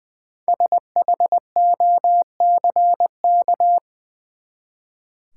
Morse code, heard as a keyed tone, about 20 wpm, 700 Hz. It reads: SHOCK
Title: Five Letter Words - 20wpm